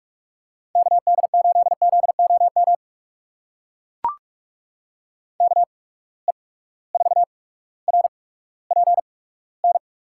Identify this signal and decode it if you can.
Morse code, heard as a keyed tone, about 44 wpm, spaced as for 19 wpm, 700 Hz. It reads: KD8ZOM KE4RPN
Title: USA Call Signs - 6 Characters - 8x Character Spacing - 45wpm